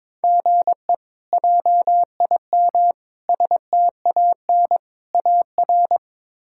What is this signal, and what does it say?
Morse code, heard as a keyed tone, about 22 wpm, 700 Hz. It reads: GE JIM STAN AR
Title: K1USN SST QSO Practice - 22wpm